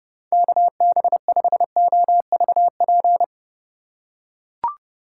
Morse code, heard as a keyed tone, about 29 wpm, 700 Hz. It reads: KB5OVP